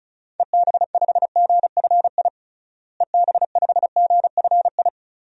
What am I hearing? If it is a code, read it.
Morse code, heard as a keyed tone, about 36 wpm, 700 Hz. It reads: EB5GFI EB5GFI
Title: DXCC-100 Call Signs - 6 Characters - Repeated 2x - 35wpm